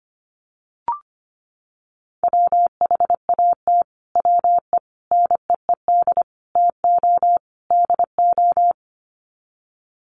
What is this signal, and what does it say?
Morse code, heard as a keyed tone, about 25 wpm, 700 Hz. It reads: WHAT WE NEED TO DO
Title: Sets of 5 Words Encore 25wpm